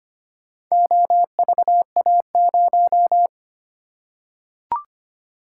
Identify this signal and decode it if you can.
Morse code, heard as a keyed tone, about 25 wpm, 700 Hz. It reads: OVA0
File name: practice / 25wpm / Four Letter-Number Combinations - 25wpm